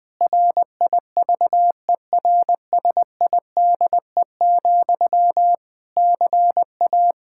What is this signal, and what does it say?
Morse code, heard as a keyed tone, about 20 wpm, 700 Hz. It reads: RIVERSIDE, CA
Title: Practice QSO - Exchange 1 - Beginning - 20wpm